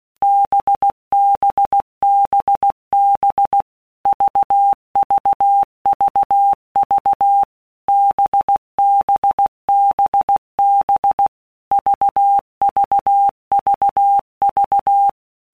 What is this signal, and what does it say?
Morse code, heard as a keyed tone, about 16 wpm, 800 Hz. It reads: BBBB VVVV BBBB VVVV
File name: Basic Letter Groups By WPM's / Groups@15 / VB46@15